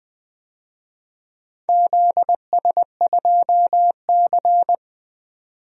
Morse code, heard as a keyed tone, about 20 wpm, 700 Hz. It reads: ZS2C
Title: DXCC-100 Call Signs - 4 Characters - 20wpm